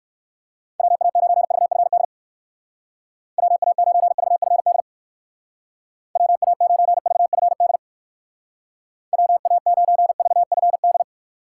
Morse code, heard as a keyed tone, about 63 wpm, 700 Hz. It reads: WA9VFD WA9VFD WA9VFD WA9VFD